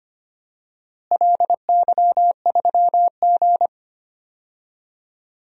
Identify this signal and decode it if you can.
Morse code, heard as a keyed tone, about 25 wpm, 700 Hz. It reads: LY3G
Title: DXCC-100 Call Signs - 4 Characters - Repeated 2x - 25wpm